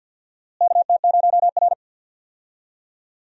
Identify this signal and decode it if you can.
Morse code, heard as a keyed tone, about 53 wpm, 700 Hz. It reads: KT0R
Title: USA Call Signs - 4 Characters - Repeated 3x - 50wpm